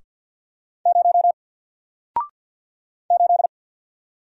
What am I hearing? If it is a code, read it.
Morse code, heard as a keyed tone, about 54 wpm, 700 Hz. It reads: 0 8
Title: Single Letter-Number 50wpm